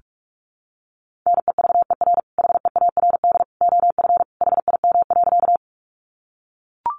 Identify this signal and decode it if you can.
Morse code, heard as a keyed tone, about 44 wpm, 700 Hz. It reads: NEVER HEARD OF HIM.
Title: Sentences from Top 400 Words 45wpm